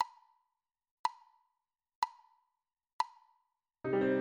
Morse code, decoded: none